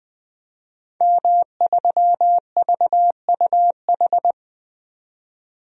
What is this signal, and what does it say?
Morse code, heard as a keyed tone, about 20 wpm, 700 Hz. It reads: M3VUH
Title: DXCC-100 Call Signs - 5 Characters - 20wpm